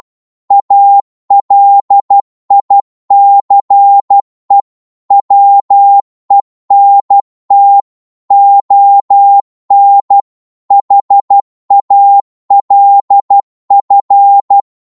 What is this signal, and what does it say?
Morse code, heard as a keyed tone, about 12 wpm, 800 Hz. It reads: ALICE WENT ON HALF
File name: Morse / psg-12wpm